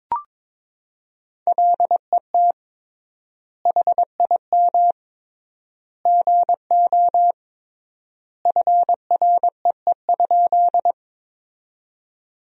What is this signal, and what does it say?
Morse code, heard as a keyed tone, about 22 wpm, 700 Hz. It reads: LET HIM GO FREE?
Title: 3x Word-Spacing - Sentences from Top 500 Words - 22wpm